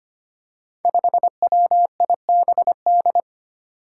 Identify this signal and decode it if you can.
Morse code, heard as a keyed tone, about 25 wpm, 700 Hz. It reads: 5WIBD
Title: Five Challenging Letter-Number Combinations 25wpm